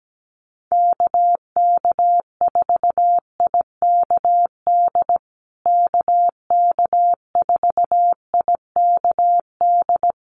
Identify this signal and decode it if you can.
Morse code, heard as a keyed tone, about 17 wpm, 700 Hz. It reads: KK4IKD KK4IKD
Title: USA Call Signs Repeated 2x 17wpm